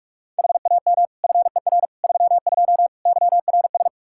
Morse code, heard as a keyed tone, about 43 wpm, 700 Hz. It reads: HAM FER 32 YRS
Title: Practice QSO - Exchange 3 - More Info - 45wpm